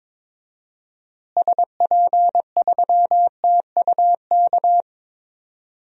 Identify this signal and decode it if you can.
Morse code, heard as a keyed tone, about 22 wpm, 700 Hz. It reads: SP3TUK
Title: DXCC-100 Call Signs - 6 Characters - Repeated 3x - 22wpm